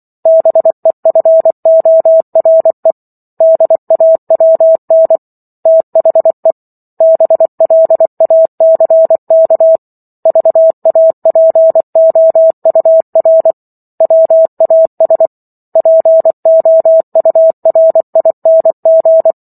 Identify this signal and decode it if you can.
Morse code, heard as a keyed tone, about 24 wpm, 650 Hz. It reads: BEFORE DAWN THE BLACK VAPOUR WAS POURING